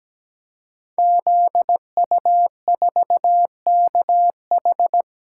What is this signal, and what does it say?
Morse code, heard as a keyed tone, about 17 wpm, 700 Hz. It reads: ZU4KH
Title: Code Groups - 5 Letter:Numbers - 17wpm